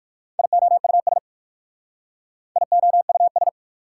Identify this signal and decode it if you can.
Morse code, heard as a keyed tone, about 52 wpm, 690 Hz. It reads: IOUS IOUS
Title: Speed-Racing Prefixes and Suffixes 45wpm